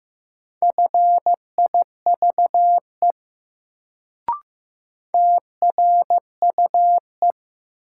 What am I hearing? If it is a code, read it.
Morse code, heard as a keyed tone, about 15 wpm, 700 Hz. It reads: FIVE TRUE
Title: Sentences from Top 400 Words - New Word Drill - 15wpm